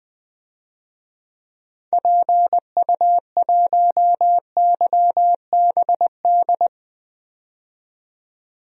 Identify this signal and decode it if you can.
Morse code, heard as a keyed tone, about 20 wpm, 700 Hz. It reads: PU1YBD
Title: DXCC-100 Call Signs - 6 Characters - 20wpm